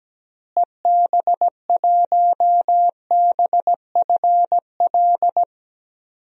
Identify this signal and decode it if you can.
Morse code, heard as a keyed tone, about 17 wpm, 700 Hz. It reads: EB1BFL
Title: DXCC-100 Call Signs - 6 Characters - Repeated 2x - 17wpm